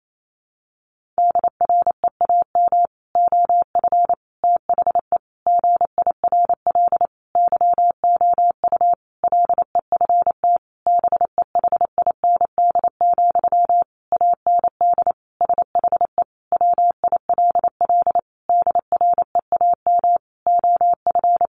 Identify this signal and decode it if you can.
Morse code, heard as a keyed tone, about 28 wpm, 700 Hz. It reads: DREAM OF THE GIRL YOU LEFT BEHIND, AND SHE WILL DREAM OF
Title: Sentences from Top 1500 Words 28wpm